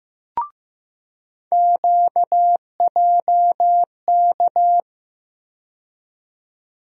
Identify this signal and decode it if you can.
Morse code, heard as a keyed tone, about 15 wpm, 700 Hz. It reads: QJK
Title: USA Call Sign Postfixes - 3 Letters 15wpm